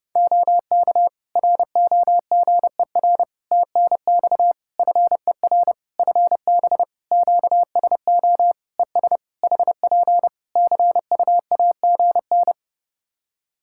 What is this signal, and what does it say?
Morse code, heard as a keyed tone, about 30 wpm, 700 Hz. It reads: OK ROGER TNX FER FB QSO ES HP CUAGN
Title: Practice QSO - Exchange 4 - Ending - 30wpm